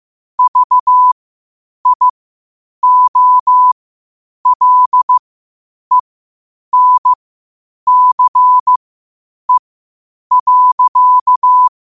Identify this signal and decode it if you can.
Morse code, heard as a keyed tone, about 15 wpm, spaced as for 8 wpm, 1000 Hz. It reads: VIOLENCE.
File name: qotdfiles / 8wpm